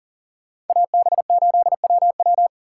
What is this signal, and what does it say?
Morse code, heard as a keyed tone, about 52 wpm, 700 Hz. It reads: AB8WW